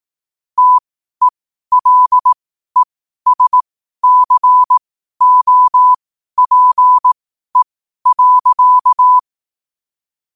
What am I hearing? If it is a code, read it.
Morse code, heard as a keyed tone, about 18 wpm, spaced as for 12 wpm, 1000 Hz. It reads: TELESCOPE.
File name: qotdfiles / 12wpm